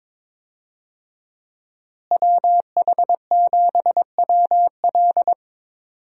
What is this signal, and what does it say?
Morse code, heard as a keyed tone, about 21 wpm, 700 Hz. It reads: WH7WL
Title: USA Call Signs - 5 Characters - Repeated 3x - 22wpm